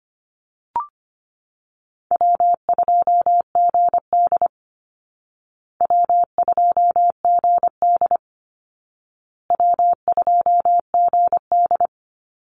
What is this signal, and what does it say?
Morse code, heard as a keyed tone, about 25 wpm, 700 Hz. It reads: W2GD W2GD W2GD
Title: Easy Call Signs Repeated 3x - 25wpm